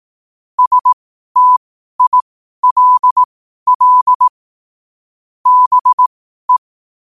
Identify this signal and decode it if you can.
Morse code, heard as a keyed tone, about 18 wpm, spaced as for 12 wpm, 1000 Hz. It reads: STILL BE
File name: qotdfiles / 12wpm